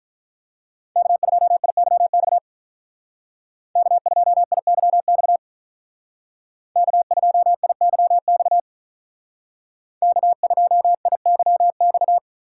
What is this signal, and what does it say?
Morse code, heard as a keyed tone, about 44 wpm, 700 Hz. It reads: K2IYX K2IYX K2IYX K2IYX